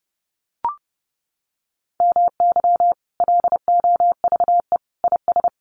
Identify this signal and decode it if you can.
Morse code, heard as a keyed tone, about 30 wpm, 700 Hz. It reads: MY LOVE IS
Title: Sentences from Top 1500 Words 30wpm